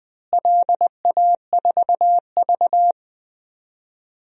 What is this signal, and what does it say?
Morse code, heard as a keyed tone, about 19 wpm, 700 Hz. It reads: LA4V
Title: DXCC-100 Call Signs - 4 Characters - Repeated 2x - 20wpm